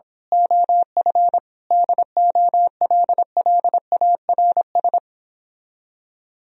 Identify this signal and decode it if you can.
Morse code, heard as a keyed tone, about 26 wpm, 700 Hz. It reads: OF DOLLARS
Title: Speed-Racing - Sets of 5 Words 22wpm